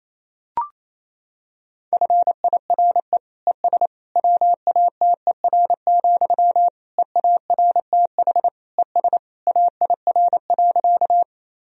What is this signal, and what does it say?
Morse code, heard as a keyed tone, about 28 wpm, 700 Hz. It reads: FIRE ES WATER, EARTH ES AIR.
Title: Sentences from Top 400 Words and Abbreviations 28wpm